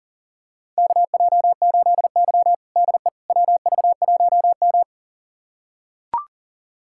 Morse code, heard as a keyed tone, about 40 wpm, 700 Hz. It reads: KJ8Y DE WV1M
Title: USA Callsign DE Callsign - 4 Characters - 40wpm